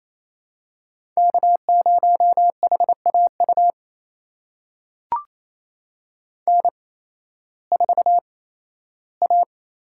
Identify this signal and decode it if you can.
Morse code, heard as a keyed tone, about 28 wpm, spaced as for 15 wpm, 700 Hz. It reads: K0HAU N4A
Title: USA Call Signs - 5 Characters - 8x Character Spacing - 28wpm